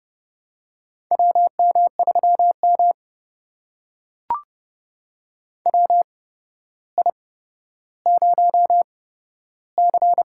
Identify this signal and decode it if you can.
Morse code, heard as a keyed tone, about 30 wpm, spaced as for 13 wpm, 700 Hz. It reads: WM3M WI0C
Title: USA Call Signs - 4 Characters - 8x Character Spacing - 30wpm